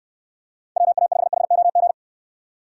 Morse code, heard as a keyed tone, about 69 wpm, 700 Hz. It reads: WA5HOG